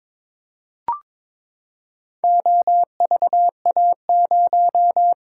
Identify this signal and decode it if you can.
Morse code, heard as a keyed tone, about 22 wpm, 700 Hz. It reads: OVA0
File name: practice / 22wpm / Four Letter-Number Combinations - 22wpm